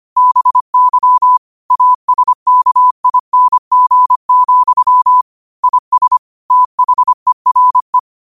Morse code, heard as a keyed tone, about 25 wpm, 1000 Hz. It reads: BY ASKING, IS THERE
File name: qotdfiles / 25wpm